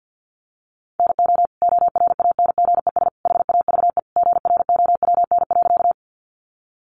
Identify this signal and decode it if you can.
Morse code, heard as a keyed tone, about 50 wpm, 700 Hz. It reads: NO ORANGES HAVE GROWN.
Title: Easy Phrases 50wpm